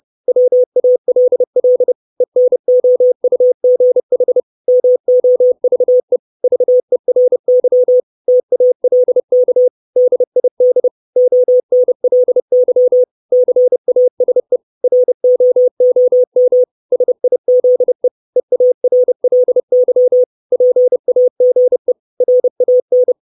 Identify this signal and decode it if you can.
Morse code, heard as a keyed tone, about 30 wpm, 500 Hz. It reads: WALL ENOUGH MOVE VERY TALK DID ONLY CASE ROOM SIZE EARLY PAGE RAN
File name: 632 Words 30WPM